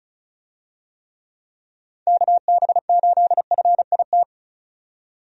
Morse code, heard as a keyed tone, about 36 wpm, 700 Hz. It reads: KB8FIT
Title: USA Call Signs - 6 Characters - 35wpm